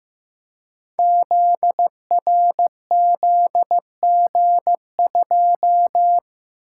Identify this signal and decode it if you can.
Morse code, heard as a keyed tone, about 15 wpm, 700 Hz. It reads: ZRZG2
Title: Code Groups - 5 Letter:Numbers - 15wpm